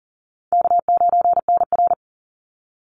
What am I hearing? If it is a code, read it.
Morse code, heard as a keyed tone, about 40 wpm, 700 Hz. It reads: K9NR
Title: Easy Call Signs Repeated 3x - 40wpm